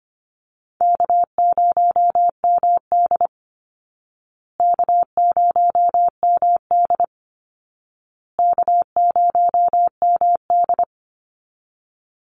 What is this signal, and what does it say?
Morse code, heard as a keyed tone, about 25 wpm, 700 Hz. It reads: K0MD K0MD K0MD
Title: Call Signs Repeated 2x - 25wpm